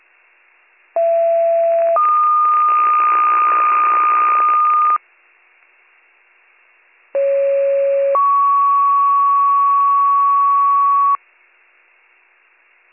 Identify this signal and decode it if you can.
Two-tone page, heard as a one-second tone, then a three-second tone, about 670 then 1220 Hz, about 550 then 1120 Hz. It is heard from 1.0 to 11.2 s.